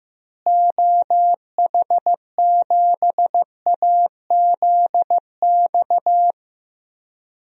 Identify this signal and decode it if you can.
Morse code, heard as a keyed tone, about 15 wpm, 700 Hz. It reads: OH7AZX
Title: DXCC-100 Call Signs - 6 Characters - Repeated 2x - 15wpm